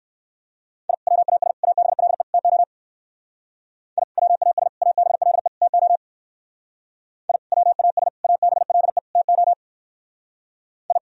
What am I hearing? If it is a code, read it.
Morse code, heard as a keyed tone, about 58 wpm, 700 Hz. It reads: I WAS ABLE TO I WAS ABLE TO I WAS ABLE TO I